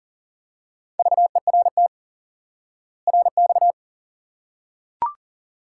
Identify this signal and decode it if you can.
Morse code, heard as a keyed tone, about 40 wpm, 700 Hz. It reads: VERT RX
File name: IQE Course 3 - Lesson 89 - YAGI - Two Elements - 40wpm